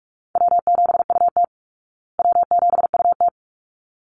Morse code, heard as a keyed tone, about 44 wpm, 700 Hz. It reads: W7UT W7UT
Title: Call Signs Repeated 3x - 45wpm